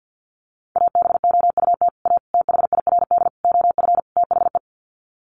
Morse code, heard as a keyed tone, about 48 wpm, 700 Hz. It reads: ABOUT A THIRD OF THE
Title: Sets of 5 Words Encore2 50wpm